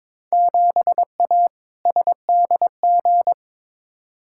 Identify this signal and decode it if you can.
Morse code, heard as a keyed tone, about 22 wpm, 700 Hz. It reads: 7A SDG
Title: Field Day Exchanges Encore - 22wpm